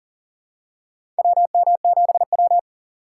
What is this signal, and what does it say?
Morse code, heard as a keyed tone, about 40 wpm, 700 Hz. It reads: WM7W
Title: USA Call Signs - 3x Character Spacing - 40wpm